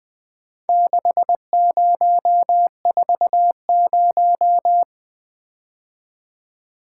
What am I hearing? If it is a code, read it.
Morse code, heard as a keyed tone, about 20 wpm, 700 Hz. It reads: 6040
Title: Four Letter-Number Combinations - 20wpm